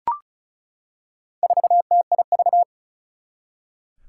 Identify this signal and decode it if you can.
Morse code, heard as a keyed tone, about 36 wpm, 690 Hz. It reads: 4TIV
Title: Four Challenging Letter-Number Combinations 35wpm